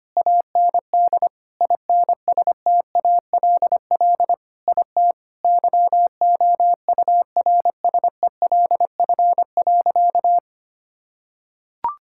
Morse code, heard as a keyed tone, about 25 wpm, 700 Hz. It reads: AND INSTALL IT YOURSELF.
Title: Sentences from Top 1600 Words 25wpm